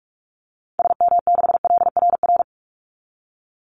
Morse code, heard as a keyed tone, about 44 wpm, 700 Hz. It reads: SM6LRR
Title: Call Signs Repeated 3x - 45wpm